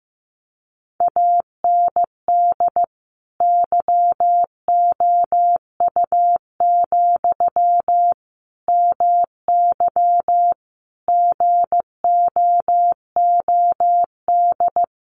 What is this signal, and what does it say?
Morse code, heard as a keyed tone, about 15 wpm, 700 Hz. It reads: AND YOU, MY GOOD
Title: Sentences from Top 1100 Words 15wpm